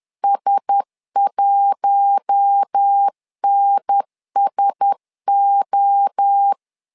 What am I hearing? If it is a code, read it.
Morse code, heard as a keyed tone, about 10 wpm, 800 Hz. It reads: S1NSO
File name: LEZIONI / 4 Q1S 10wpm 2133